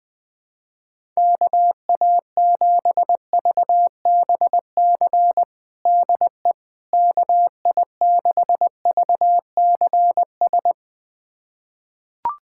Morse code, heard as a keyed tone, about 21 wpm, 700 Hz. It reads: KA7VBC DE KI6VCS
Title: USA Callsign DE Callsign - 6 Characters - 20wpm